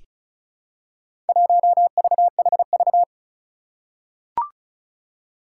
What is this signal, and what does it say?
Morse code, heard as a keyed tone, about 36 wpm, 700 Hz. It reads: 1VHV
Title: Four Challenging Letter-Number Combinations 35wpm